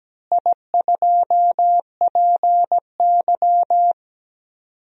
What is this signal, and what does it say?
Morse code, heard as a keyed tone, about 17 wpm, 700 Hz. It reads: I2PY
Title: DXCC-100 Call Signs - 4 Characters - Repeated 3x - 17wpm